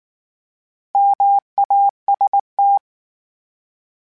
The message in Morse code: MAST

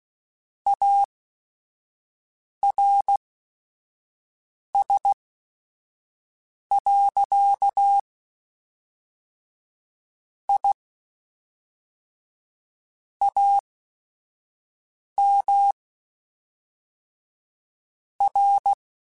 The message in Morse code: ARS. I AM R

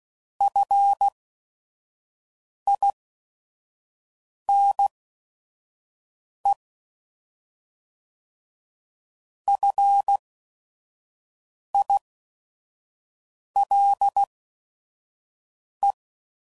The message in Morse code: FINE FILE

